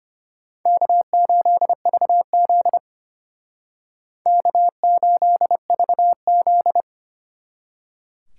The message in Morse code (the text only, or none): K8VZ K8VZ